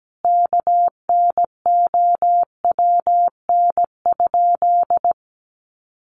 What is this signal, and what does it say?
Morse code, heard as a keyed tone, about 17 wpm, 700 Hz. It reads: KNOWN?
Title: Sentences from Top 1300 Words 17wpm